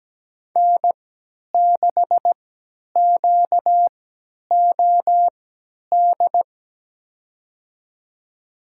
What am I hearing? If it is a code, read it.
Morse code, heard as a keyed tone, about 17 wpm, spaced as for 10 wpm, 700 Hz. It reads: N6QOD
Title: USA Call Signs - 3x Character Spacing - 17wpm